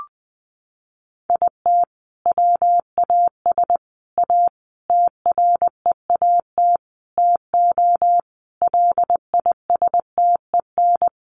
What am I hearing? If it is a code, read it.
Morse code, heard as a keyed tone, about 20 wpm, 700 Hz. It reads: IT WAS A TREAT TO LISTEN
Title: Sentences from Top 1300 Words 20wpm